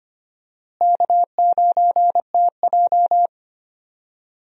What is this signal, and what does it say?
Morse code, heard as a keyed tone, about 25 wpm, 700 Hz. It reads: K9TJ
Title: USA Call Signs - 4 Characters - Repeated 2x - 25wpm